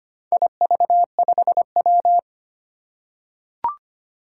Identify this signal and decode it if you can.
Morse code, heard as a keyed tone, about 25 wpm, 700 Hz. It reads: IV5W
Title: Four Challenging Letter-Number Combinations 25wpm